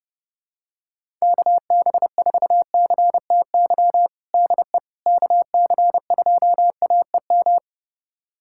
Morse code, heard as a keyed tone, about 30 wpm, 700 Hz. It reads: KB4CTY DE KC2AEM
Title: USA Callsign DE Callsign 30wpm